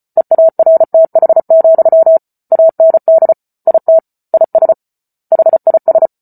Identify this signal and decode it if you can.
Morse code, heard as a keyed tone, about 34 wpm, 650 Hz. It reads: EARTH, AND IT IS HIS
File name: Book Two / 08-Chapter 8 - 34 WPM